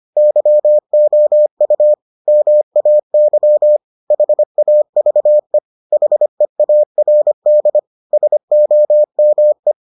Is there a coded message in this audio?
Yes: Morse code from 0.2 to 9.7 s, heard as a keyed tone, about 25 wpm, 600 Hz. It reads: YOU MAY HAVE HEARD SOME